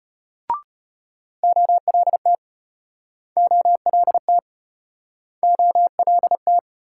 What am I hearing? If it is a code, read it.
Morse code, heard as a keyed tone, about 35 wpm, 700 Hz. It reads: OLT OLT OLT